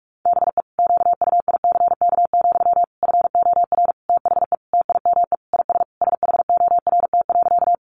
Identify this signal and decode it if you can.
Morse code, heard as a keyed tone, about 44 wpm, 700 Hz. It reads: BE QUICK, FOR THE TIME IS SHORT.
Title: Sentences from Top 500 Words 45wpm